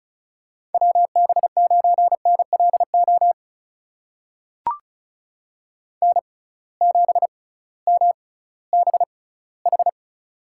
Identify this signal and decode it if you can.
Morse code, heard as a keyed tone, about 36 wpm, spaced as for 19 wpm, 700 Hz. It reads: WB9NLO N7MBH
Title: USA Call Signs - 6x Character Spacing - 35wpm